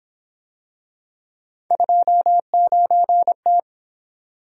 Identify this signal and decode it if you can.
Morse code, heard as a keyed tone, about 28 wpm, 700 Hz. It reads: 29T